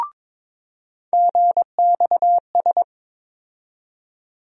Morse code, heard as a keyed tone, about 22 wpm, 700 Hz. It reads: GXS
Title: USA Call Sign Postfixes - 3 Letters 22wpm